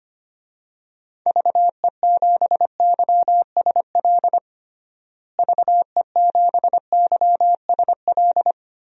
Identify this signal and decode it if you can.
Morse code, heard as a keyed tone, about 25 wpm, 700 Hz. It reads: VE7YSL VE7YSL
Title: DXCC-100 Call Signs - 6 Characters - Repeated 2x - 25wpm